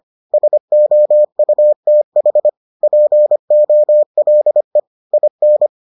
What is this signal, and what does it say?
Morse code, heard as a keyed tone, about 25 wpm, 600 Hz. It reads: SOUTH POLE IN